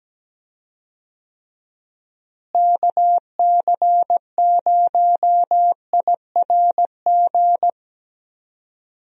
Morse code, heard as a keyed tone, about 17 wpm, 700 Hz. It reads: KC0IRG
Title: USA Call Signs - 6 Characters - Repeated 2x - 17wpm